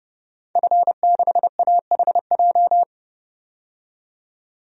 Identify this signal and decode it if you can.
Morse code, heard as a keyed tone, about 32 wpm, 700 Hz. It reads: F6AHJ
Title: DXCC-100 Call Signs - 5 Characters - 30wpm